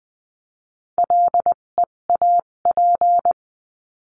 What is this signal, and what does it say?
Morse code, heard as a keyed tone, about 20 wpm, spaced as for 17 wpm, 700 Hz. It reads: LEAP
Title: 257 Lesson 42b - Words 14-20wpm